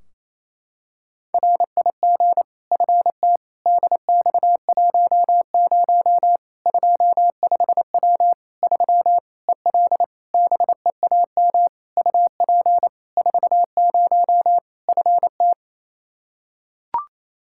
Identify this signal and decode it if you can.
Morse code, heard as a keyed tone, about 28 wpm, 700 Hz. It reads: RIG FT DX10 25W 3 EL BEAM UP 40 FT
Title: Practice QSO - Exchange 2 - Info - 28wpm